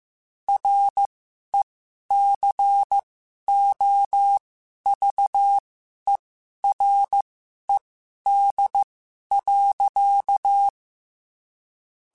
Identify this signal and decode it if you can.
Morse code, heard as a keyed tone, about 15 wpm, spaced as for 11 wpm, 800 Hz. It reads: RECOVERED.